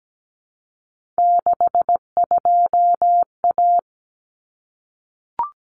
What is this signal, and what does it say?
Morse code, heard as a keyed tone, about 17 wpm, 700 Hz. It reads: 62A
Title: Common Cut Numbers - 3-Digit 17wpm